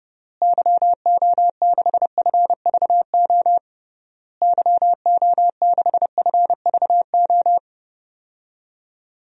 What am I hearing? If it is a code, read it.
Morse code, heard as a keyed tone, about 32 wpm, 700 Hz. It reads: YO6FVO YO6FVO
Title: DXCC-100 Call Signs - 6 Characters - Repeated 3x - 30wpm